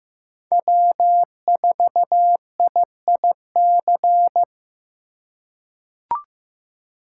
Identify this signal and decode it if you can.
Morse code, heard as a keyed tone, about 15 wpm, 700 Hz. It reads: W4IIC
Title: K1USN SST QSO Practice - 15wpm